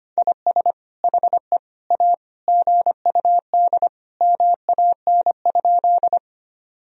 Morse code, heard as a keyed tone, about 25 wpm, 700 Hz. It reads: IS HE A GUD MAN?
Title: Sentences from Top 200 Words and Abbreviations 25wpm